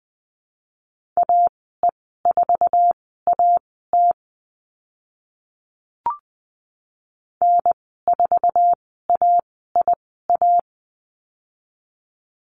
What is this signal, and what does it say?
Morse code, heard as a keyed tone, about 20 wpm, spaced as for 15 wpm, 700 Hz. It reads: AE4AT N4AIA
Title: 048 Lesson 9g - Number 4 - Call Signs 20wpm